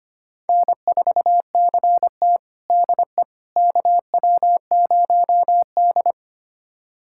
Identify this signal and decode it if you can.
Morse code, heard as a keyed tone, about 25 wpm, 700 Hz. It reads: N4CT DE KW0D
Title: USA Callsign DE Callsign - 4 Characters - 25wpm